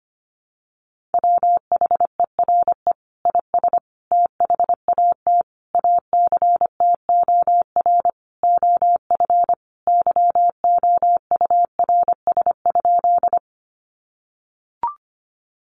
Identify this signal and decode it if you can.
Morse code, heard as a keyed tone, about 25 wpm, 700 Hz. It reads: WHERE IS THAT ACTOR OF YOURS?
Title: Sentences from Top 1500 Words 25wpm